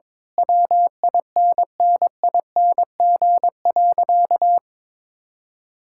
Morse code, heard as a keyed tone, about 21 wpm, 700 Hz. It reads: WINNING.
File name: practice / 22wpm / Sentences with 8-Word Phrases - 22wpm